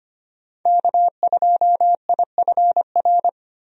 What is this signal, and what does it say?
Morse code, heard as a keyed tone, about 25 wpm, 700 Hz. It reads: K2IFR
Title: USA Call Signs - 5 Characters - Repeated 3x - 25wpm